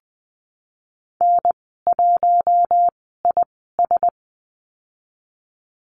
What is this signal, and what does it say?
Morse code, heard as a keyed tone, about 19 wpm, spaced as for 14 wpm, 700 Hz. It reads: N1IS
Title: 140 Lesson 23g - Letter f - Call Signs 20wpm